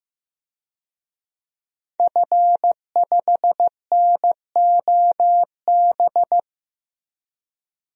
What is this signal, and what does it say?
Morse code, heard as a keyed tone, about 15 wpm, 700 Hz. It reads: F5NOB
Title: DXCC-100 Call Signs - 5 Characters - 15wpm